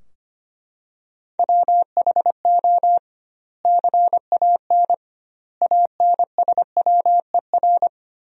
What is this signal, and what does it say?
Morse code, heard as a keyed tone, about 25 wpm, 700 Hz. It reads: WHO CAN ANSWER
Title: 2x Word-Spacing - Sentences from Top 300 Words - 25wpm